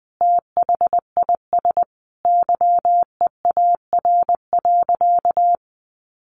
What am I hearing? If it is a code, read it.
Morse code, heard as a keyed tone, about 20 wpm, 700 Hz. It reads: THIS YEAR.
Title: Sentences from Top 1500 Words 20wpm